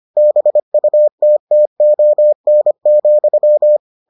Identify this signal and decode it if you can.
Morse code, heard as a keyed tone, about 25 wpm, 600 Hz. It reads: BUTTON,